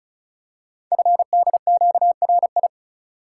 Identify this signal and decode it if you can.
Morse code, heard as a keyed tone, about 36 wpm, 700 Hz. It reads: FDQRI
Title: Code Groups - 5 Letters - 35wpm